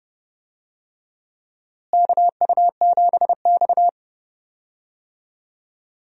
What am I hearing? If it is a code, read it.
Morse code, heard as a keyed tone, about 30 wpm, 700 Hz. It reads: KU7X
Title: USA Call Signs - 4 Characters - Repeated 2x - 30wpm